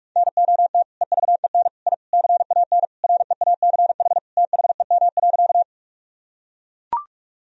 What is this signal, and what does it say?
Morse code, heard as a keyed tone, about 42 wpm, 700 Hz. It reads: NOT EVEN I CAN REACH THEM.
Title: Sentences from Top 1000 Words - Review Part 2 - 45wpm